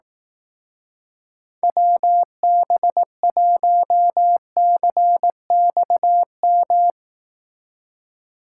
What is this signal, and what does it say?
Morse code, heard as a keyed tone, about 18 wpm, 700 Hz. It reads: WB1CXM
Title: Speed Racing - USA 2x3 Call Signs 15wpm